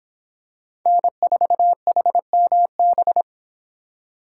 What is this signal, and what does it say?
Morse code, heard as a keyed tone, about 26 wpm, 700 Hz. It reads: N4HMB